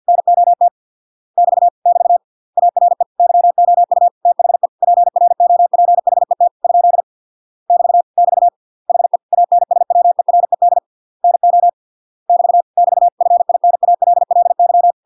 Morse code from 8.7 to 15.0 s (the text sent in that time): HE ANSWERED NO <BT> <BT> FINALLY